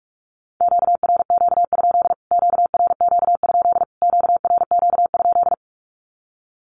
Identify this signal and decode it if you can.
Morse code, heard as a keyed tone, about 44 wpm, 700 Hz. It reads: QRQ? QRQ? QRQ?
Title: Q-Codes Repeated 3x 45wpm